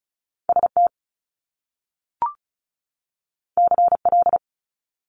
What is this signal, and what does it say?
Morse code, heard as a keyed tone, about 35 wpm, 700 Hz. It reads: S T C L
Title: Top Two Letter Combinations in Words 35wpm